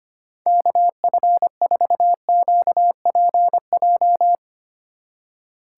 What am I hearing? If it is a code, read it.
Morse code, heard as a keyed tone, about 25 wpm, 700 Hz. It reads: KF4QPJ
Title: USA Call Signs - 2x Character Spacing - 25wpm